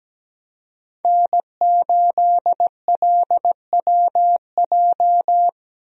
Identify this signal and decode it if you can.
Morse code, heard as a keyed tone, about 17 wpm, 700 Hz. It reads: N8LWJ
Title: USA Call Signs - 5 Characters - Repeated 3x - 17wpm